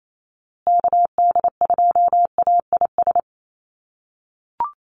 Morse code, heard as a keyed tone, about 28 wpm, 700 Hz. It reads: KD2AIS
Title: USA Call Signs Repeated 2x 28wpm